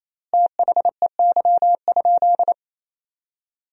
Morse code, heard as a keyed tone, about 27 wpm, 700 Hz. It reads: THEY?
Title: 3x Word-Spacing - Sentences from Top 200 Words - 28wpm